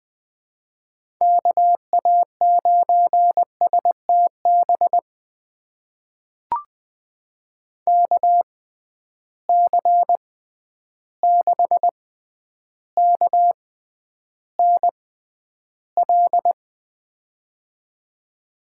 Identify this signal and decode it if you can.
Morse code, heard as a keyed tone, about 21 wpm, spaced as for 10 wpm, 700 Hz. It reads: KA9STB KC6KNL
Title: USA Call Signs - 6 Characters - 6x Character Spacing - 20wpm